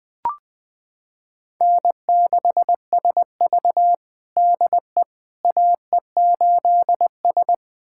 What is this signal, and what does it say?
Morse code, heard as a keyed tone, about 21 wpm, 700 Hz. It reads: N6SV DE AE8S
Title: USA Callsign DE Callsign - 4 Characters - 20wpm